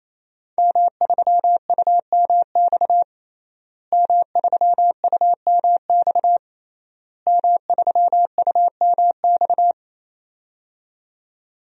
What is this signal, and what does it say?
Morse code, heard as a keyed tone, about 28 wpm, 700 Hz. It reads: M3UMX M3UMX M3UMX
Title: DXCC-100 Call Signs - 5 Characters - Repeated 3x - 28wpm